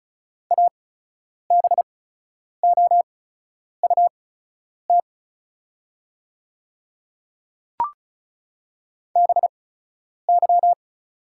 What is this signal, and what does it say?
Morse code, heard as a keyed tone, about 36 wpm, spaced as for 10 wpm, 700 Hz. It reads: ABOUT BY
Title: 8x Character Spacing - Top 100 Words - 35wpm